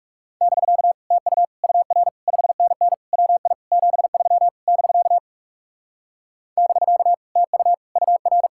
Morse code, heard as a keyed tone, about 38 wpm, 700 Hz. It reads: <BK> TU UR 5NN WI 73 <BK> <BK> TU UR